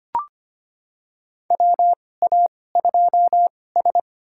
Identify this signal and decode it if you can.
Morse code, heard as a keyed tone, about 25 wpm, spaced as for 18 wpm, 700 Hz. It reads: WA2S
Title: USA Call Signs - 4 Characters - 2x Character Spacing - 25wpm